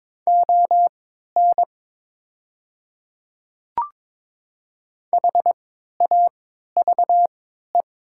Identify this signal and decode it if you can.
Morse code, heard as a keyed tone, about 22 wpm, spaced as for 12 wpm, 700 Hz. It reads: ON HAVE
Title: 3x Character Spacing - Top 300 Words - 22wpm